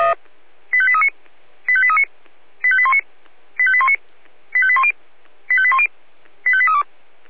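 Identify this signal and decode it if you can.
Five-tone selective call (ZVEI), heard as quick runs of five tones, tones of about 70 ms. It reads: 85839 85839 85819 85819 85810 85810 85832
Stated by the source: ZVEI five-tone selective call received on an SDR.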